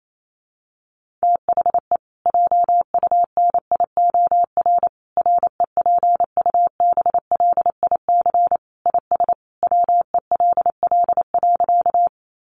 Morse code, heard as a keyed tone, about 28 wpm, 700 Hz. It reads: THE JUNIOR REPUBLIC IS WELL.
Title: Sentences from Top 1500 Words 28wpm